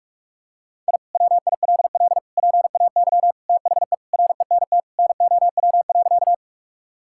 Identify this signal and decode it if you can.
Morse code, heard as a keyed tone, about 46 wpm, 700 Hz. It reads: I WILL PAY THE RENT NOW.
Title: Sentences from Top 1000 Words - Review Part 5 - 45wpm